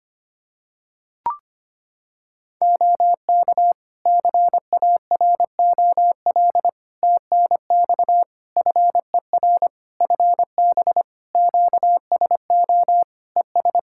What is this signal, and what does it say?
Morse code, heard as a keyed tone, about 25 wpm, 700 Hz. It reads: OK CAROL TNX FER FB QSO ES
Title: Practice QSO - Exchange 4 - Ending - 25wpm